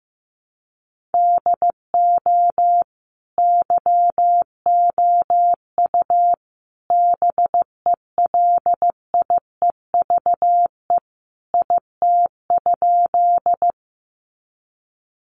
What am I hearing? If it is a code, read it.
Morse code, heard as a keyed tone, about 15 wpm, 700 Hz. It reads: DO YOU BELIEVE IT?
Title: Sentences from Top 1000 Words 15wpm